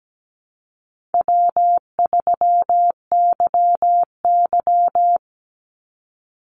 Morse code, heard as a keyed tone, about 17 wpm, 700 Hz. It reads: W3YY
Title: Easy Call Signs - 17wpm